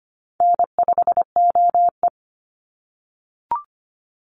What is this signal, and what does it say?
Morse code, heard as a keyed tone, about 25 wpm, 700 Hz. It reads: N5OE
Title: Easy Call Signs - 25wpm